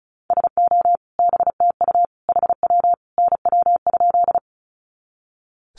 Morse code, heard as a keyed tone, about 34 wpm, 700 Hz. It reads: SO BTU HW NW?
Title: Common QSO Phrases 35wpm